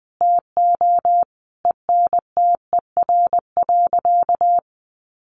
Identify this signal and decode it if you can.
Morse code, heard as a keyed tone, about 20 wpm, 700 Hz. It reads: TO ENTER.
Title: Sentences from Top 1300 Words 20wpm